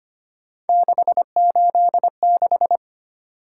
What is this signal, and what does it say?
Morse code, heard as a keyed tone, about 25 wpm, 700 Hz. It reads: 686
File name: Three Letter-Number Combinations - 25wpm